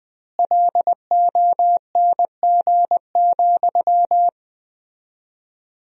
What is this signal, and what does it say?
Morse code, heard as a keyed tone, about 20 wpm, 700 Hz. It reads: LONG,
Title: 4x Word-Spacing - Sentences from Top 500 Words - 20wpm